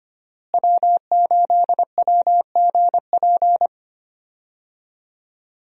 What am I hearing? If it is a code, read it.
Morse code, heard as a keyed tone, about 25 wpm, 700 Hz. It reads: W8WGP
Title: USA Call Signs - 5 Characters - Repeated 3x - 25wpm